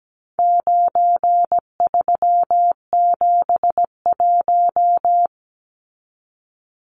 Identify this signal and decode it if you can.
Morse code, heard as a keyed tone, about 17 wpm, 700 Hz. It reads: 9371
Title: Numbers - 4-Digit 17wpm